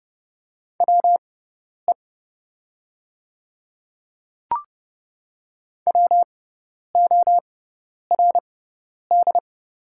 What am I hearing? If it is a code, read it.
Morse code, heard as a keyed tone, about 31 wpm, spaced as for 10 wpm, 700 Hz. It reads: WE WORD
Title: 6x Character Spacing - Top 100 Words - 30wpm